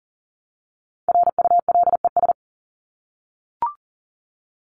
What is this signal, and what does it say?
Morse code, heard as a keyed tone, about 39 wpm, 700 Hz. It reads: RULES
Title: ES:S Suffix - Top 100 Words with Base Words 40wpm